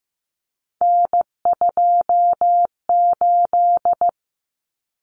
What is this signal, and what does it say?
Morse code, heard as a keyed tone, about 15 wpm, 700 Hz. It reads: N28
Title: Common Cut Numbers - 3-Digit 15wpm